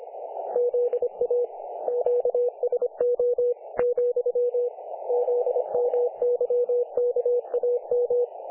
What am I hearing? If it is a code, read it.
Morse code, heard as a keyed tone, about 25 wpm, 500 Hz. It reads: ZA QSO , ZMYKAM
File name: dlazaawansowanych / nagraniazpasma / websdr recording 3564.7kHz